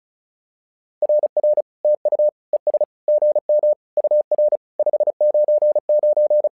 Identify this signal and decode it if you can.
Morse code, heard as a keyed tone, about 36 wpm, 600 Hz. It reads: RR TU ES GM UR 599
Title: SOTA QSO Practice 35wpm